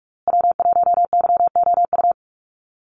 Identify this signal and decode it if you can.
Morse code, heard as a keyed tone, about 44 wpm, 700 Hz. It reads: W1YOU
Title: USA Call Signs Repeated 3x 45wpm